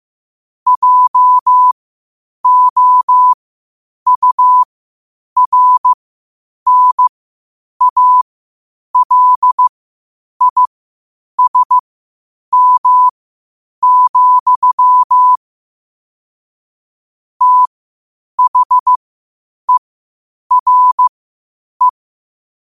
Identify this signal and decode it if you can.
Morse code, heard as a keyed tone, about 15 wpm, spaced as for 8 wpm, 1000 Hz. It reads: JOURNALISM, THERE
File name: qotdfiles / 8wpm